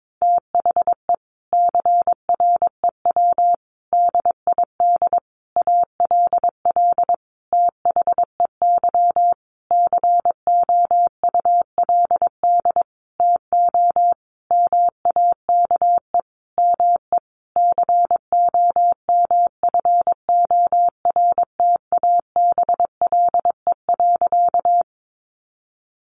Morse code, heard as a keyed tone, about 22 wpm, 700 Hz. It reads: THE CREW DID ALL THEY COULD TO MAKE ME COMFORTABLE.
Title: Sentences from Top 1500 Words 22wpm